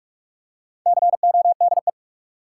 Morse code, heard as a keyed tone, about 46 wpm, 700 Hz. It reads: CODE
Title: Four Letter Words - 45wpm